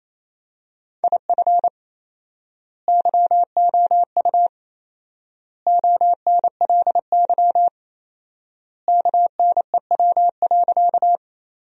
Morse code, heard as a keyed tone, about 28 wpm, 700 Hz. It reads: IF YOU ONLY KNEW.
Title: 4x Word-Spacing - Sentences from Top 400 Words - 28wpm